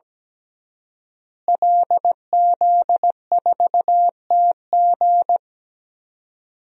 Morse code, heard as a keyed tone, about 17 wpm, 700 Hz. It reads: LZ4TG
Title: DXCC-100 Call Signs - 5 Characters - Repeated 2x - 17wpm